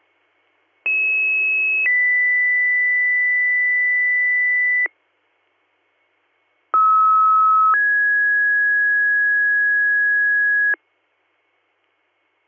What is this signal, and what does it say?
Two-tone page, heard as a one-second tone, then a three-second tone, about 2600 then 2000 Hz, about 1290 then 1650 Hz. It is heard from 0.9 to 10.7 s.